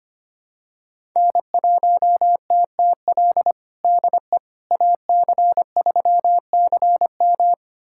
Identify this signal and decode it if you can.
Morse code, heard as a keyed tone, about 25 wpm, 700 Hz. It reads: N1TTL DE AC3CM
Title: USA Callsign DE Callsign - 5 Characters - 25wpm